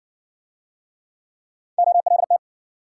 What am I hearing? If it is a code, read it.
Morse code, heard as a keyed tone, about 56 wpm, 700 Hz. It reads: OLT